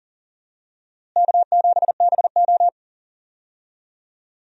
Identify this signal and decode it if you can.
Morse code, heard as a keyed tone, about 37 wpm, 700 Hz. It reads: K7BO
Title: USA Call Signs - 4 Characters - Repeated 3x - 40wpm